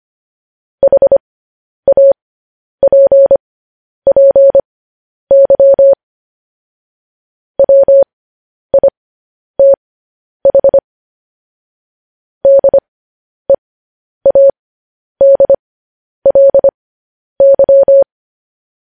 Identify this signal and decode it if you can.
Morse code, heard as a keyed tone, about 25 wpm, spaced as for 10 wpm, 550 Hz. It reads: HAPPY WITH DEADLY
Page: news headlines in morse code at 10 wpm